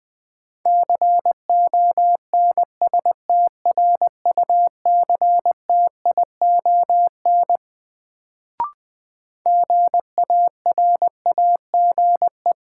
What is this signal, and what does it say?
Morse code, heard as a keyed tone, about 21 wpm, 700 Hz. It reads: CONSTRUCTION GARAGE
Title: Sentences from Top 1600 Words - New Word Drill - 20wpm